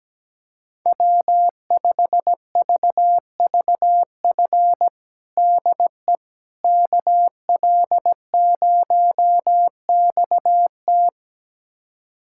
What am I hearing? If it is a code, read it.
Morse code, heard as a keyed tone, about 17 wpm, 700 Hz. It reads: W5VVF DE KL0XT
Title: USA Callsign DE Callsign - 5 Characters - 17wpm